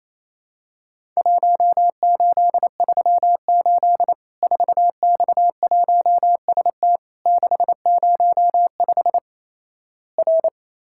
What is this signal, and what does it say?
Morse code, heard as a keyed tone, about 28 wpm, 700 Hz. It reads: 1838 4X1ST 605 R
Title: Worked All Europe DX Contest - QTC Exchange - 28wpm